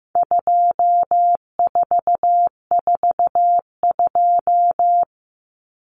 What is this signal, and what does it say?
Morse code, heard as a keyed tone, about 15 wpm, 700 Hz. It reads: 2442
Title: Numbers - 4-Digit 15wpm